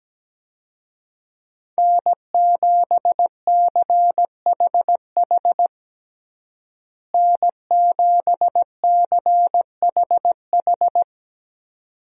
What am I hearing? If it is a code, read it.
Morse code, heard as a keyed tone, about 17 wpm, 700 Hz. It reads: N7CHH N7CHH
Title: USA Call Signs - 5 Characters - Repeated 2x - 17wpm